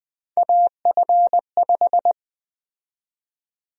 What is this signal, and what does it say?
Morse code, heard as a keyed tone, about 20 wpm, 700 Hz. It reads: AF5
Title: USA Call Sign Prefixes - 20wpm